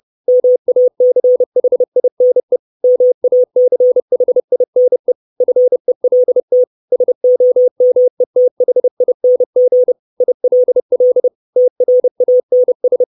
Morse code, heard as a keyed tone, about 30 wpm, 500 Hz. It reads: MACHINE MACHINE FELT SOMETHING ILL TRANS
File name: 632 Words 30WPM